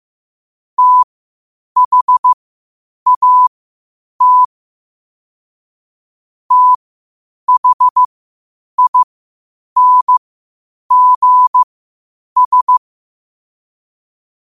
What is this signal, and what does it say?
Morse code, heard as a keyed tone, about 15 wpm, spaced as for 8 wpm, 1000 Hz. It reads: THAT THINGS
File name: qotdfiles / 8wpm